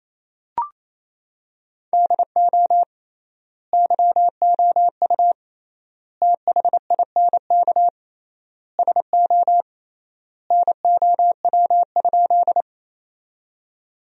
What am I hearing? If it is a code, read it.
Morse code, heard as a keyed tone, about 27 wpm, 700 Hz. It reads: DO YOU THINK SO NOW?
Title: 3x Word-Spacing - Sentences from Top 200 Words - 28wpm